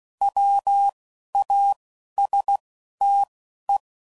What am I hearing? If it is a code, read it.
Morse code, heard as a keyed tone, about 16 wpm, spaced as for 11 wpm, 800 Hz. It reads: WASTE